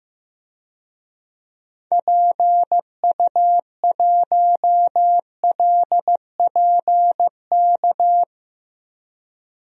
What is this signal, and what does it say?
Morse code, heard as a keyed tone, about 15 wpm, 700 Hz. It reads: PU1LPK
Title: DXCC-100 Call Signs - 6 Characters - 15wpm